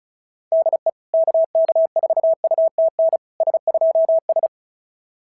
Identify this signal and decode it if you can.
Morse code, heard as a keyed tone, about 34 wpm, 650 Hz. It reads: DE KK4UTN S2S
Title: SOTA QSO Practice 35wpm